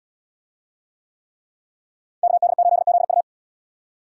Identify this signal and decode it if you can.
Morse code, heard as a keyed tone, about 69 wpm, 700 Hz. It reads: KD8ZU